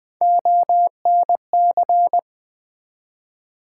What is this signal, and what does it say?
Morse code, heard as a keyed tone, about 20 wpm, 700 Hz. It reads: ONC
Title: Build A Word - 1 Letter At A Time - 4-Letter Words - 20wpm